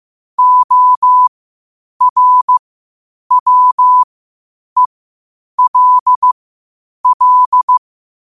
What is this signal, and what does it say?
Morse code, heard as a keyed tone, about 15 wpm, spaced as for 8 wpm, 1000 Hz. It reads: ORWELL
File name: qotdfiles / 8wpm